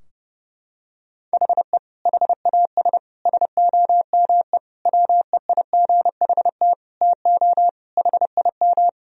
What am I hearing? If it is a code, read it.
Morse code, heard as a keyed tone, about 30 wpm, 700 Hz. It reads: HE HAS SOME WEIGHT TO HIM
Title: Sentences from Top 1000 Words - Review Part 3 - 30wpm